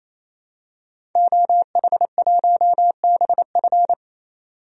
Morse code, heard as a keyed tone, about 28 wpm, 700 Hz. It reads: OH1BF
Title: DXCC-100 Call Signs - 5 Characters - 28wpm